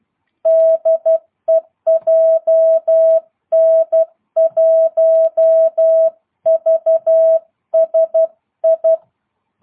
Morse code, heard as a keyed tone, about 12 wpm, 660 Hz. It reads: DE JN1VSI
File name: jh8cbh / 2022cw